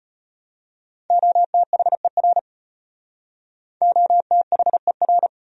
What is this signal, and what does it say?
Morse code, heard as a keyed tone, about 33 wpm, 700 Hz. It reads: OTHER OTHER